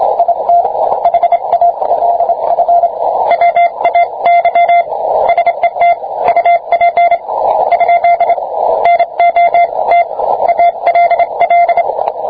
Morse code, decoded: IT HALF WAY SET UP ? NOT ALL